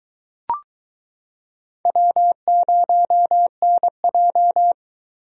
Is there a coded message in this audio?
Yes: Morse code from 1.9 to 4.7 s, heard as a keyed tone, about 23 wpm, 700 Hz. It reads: W0NJ